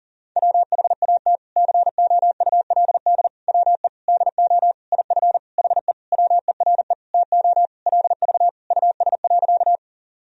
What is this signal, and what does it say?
Morse code, heard as a keyed tone, about 40 wpm, 700 Hz. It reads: WHAT COULD WE DO IF HE WERE TO LV US.
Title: Sentences from Top 400 Words and Abbreviations 40wpm